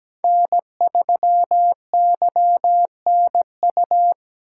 Morse code, heard as a keyed tone, about 20 wpm, 700 Hz. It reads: N3YNU
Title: Speed Racing - USA 1x3 Call Signs 17wpm